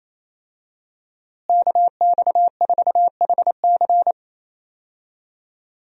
Morse code, heard as a keyed tone, about 27 wpm, 700 Hz. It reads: KX4HC
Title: USA Call Signs - 5 Characters - Repeated 2x - 28wpm